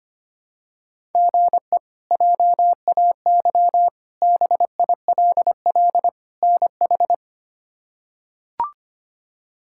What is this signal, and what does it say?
Morse code, heard as a keyed tone, about 25 wpm, 700 Hz. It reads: GE JAY BILL NH
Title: K1USN SST QSO Practice - 25wpm